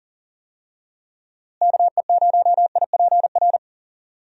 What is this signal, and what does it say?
Morse code, heard as a keyed tone, about 40 wpm, 700 Hz. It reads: KE0IPR
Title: USA Call Signs - 6 Characters - Repeated 3x - 40wpm